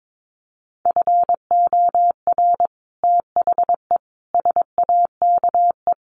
Morse code, heard as a keyed tone, about 22 wpm, 700 Hz. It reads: FOR THE SAKE
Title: Sets of 5 Words Encore 22wpm